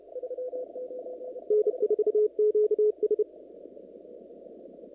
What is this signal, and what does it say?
Morse code, heard as a keyed tone, about 30 wpm, 420 Hz. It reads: N4QS
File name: CW / kt5j wpx cw-op-k5pi